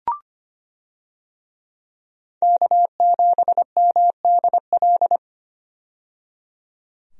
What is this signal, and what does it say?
Morse code, heard as a keyed tone, about 25 wpm, 700 Hz. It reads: K7MDL
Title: USA Call Signs - 5 Characters - 25wpm